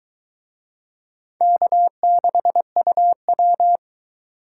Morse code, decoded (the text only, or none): K6UW